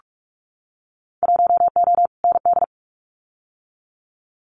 Morse code, heard as a keyed tone, about 46 wpm, 700 Hz. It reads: 1O ND
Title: Winter Field Day Exchanges 45wpm